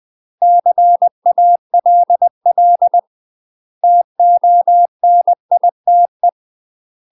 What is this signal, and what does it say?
Morse code, heard as a keyed tone, about 20 wpm, 700 Hz. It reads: CALL TONITE
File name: UK MASTER QSOs TRAINING / 20WPM / UK MASTER QSOs 0037